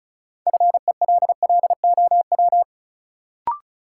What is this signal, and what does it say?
Morse code, heard as a keyed tone, about 34 wpm, 700 Hz. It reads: FELLOW
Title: Sets of 5 Words Encore - New Word Drill - 35wpm